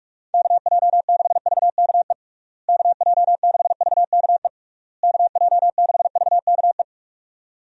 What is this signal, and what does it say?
Morse code, heard as a keyed tone, about 42 wpm, 700 Hz. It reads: KJ6VKE KJ6VKE KJ6VKE
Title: USA Call Signs - 6 Characters - Repeated 3x - 45wpm